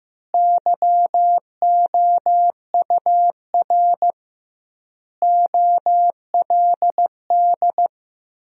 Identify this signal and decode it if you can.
Morse code, heard as a keyed tone, about 15 wpm, 700 Hz. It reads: YOUR OLD
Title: 2x Word-Spacing - Sentences from Top 500 Words - 15wpm